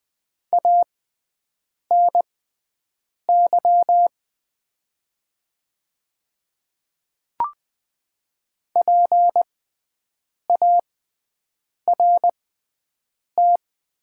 Morse code, heard as a keyed tone, about 20 wpm, spaced as for 7 wpm, 700 Hz. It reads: ANY PART
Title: 6x Character Spacing - Top 200 Words - 20wpm